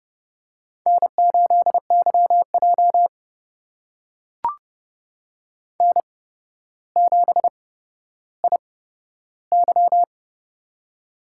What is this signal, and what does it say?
Morse code, heard as a keyed tone, about 31 wpm, spaced as for 13 wpm, 700 Hz. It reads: N8YJ N7IY
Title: USA Call Signs - 4 Characters - 8x Character Spacing - 30wpm